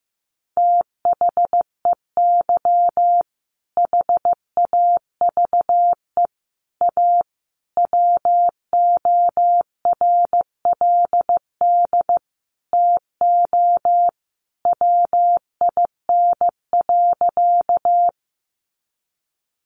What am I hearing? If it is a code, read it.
Morse code, heard as a keyed tone, about 15 wpm, 700 Hz. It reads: THEY HAVE A WORLD TO WIN.
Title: Sentences from Top 1500 Words 15wpm